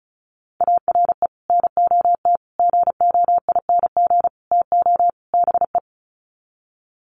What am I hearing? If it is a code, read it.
Morse code, heard as a keyed tone, about 34 wpm, 700 Hz. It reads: ARE NOT GOING TO BE
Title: Sets of 5 Words Encore 35wpm